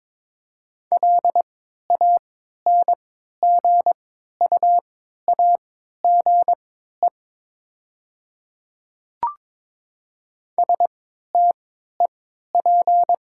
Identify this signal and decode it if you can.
Morse code, heard as a keyed tone, about 22 wpm, spaced as for 12 wpm, 700 Hz. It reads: LANGUAGE STEP
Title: 3x Character Spacing - Top 500 Words - 22wpm